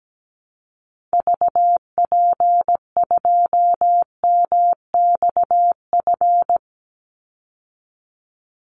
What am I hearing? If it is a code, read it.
Morse code, heard as a keyed tone, about 17 wpm, 700 Hz. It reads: VP2MXF
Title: Call Signs Repeated 3x - 17wpm